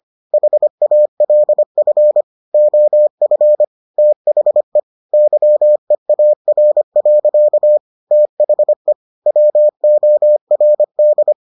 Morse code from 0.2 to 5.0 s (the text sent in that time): HALF OF THE